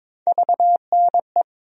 Morse code, heard as a keyed tone, about 22 wpm, 700 Hz. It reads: VNE
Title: USA Call Sign Postfixes - 3 Letters 22wpm